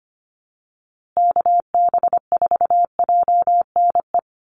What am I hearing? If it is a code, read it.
Morse code, heard as a keyed tone, about 25 wpm, 700 Hz. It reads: KB4JNE
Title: USA Call Signs Repeated 3x 25wpm